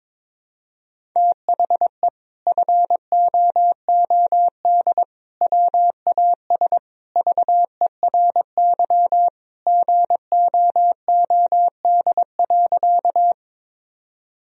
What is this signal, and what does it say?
Morse code, heard as a keyed tone, about 21 wpm, 700 Hz. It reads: THE FOOD WAS VERY GOOD.
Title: Sentences from Top 1000 Words - Review Part 1 - 22wpm